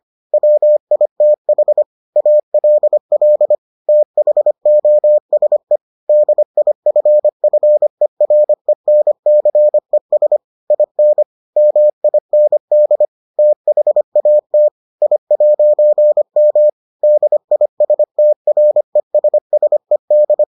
Morse code, heard as a keyed tone, about 25 wpm, 600 Hz. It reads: WITH ALL THOSE DIFFERENCES IN MIND THAT I'M DISTRESSED